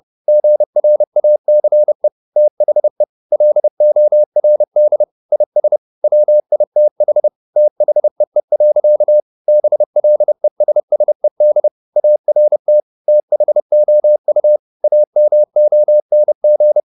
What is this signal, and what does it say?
Morse code, heard as a keyed tone, about 30 wpm, 600 Hz. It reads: GRACE THE LORD IS WITH THEE. BLESSED ART THOU AMONG
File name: the old man and the sea / the old man and the sea 1 30wpm